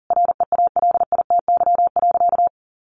Morse code, heard as a keyed tone, about 40 wpm, 700 Hz. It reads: REALITY.
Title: Sentences from Top 1200 Words 40wpm